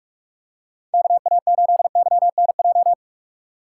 Morse code, heard as a keyed tone, about 42 wpm, 700 Hz. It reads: KA8YNJ
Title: USA Call Signs - 6 Characters - 4x Character Spacing - 45wpm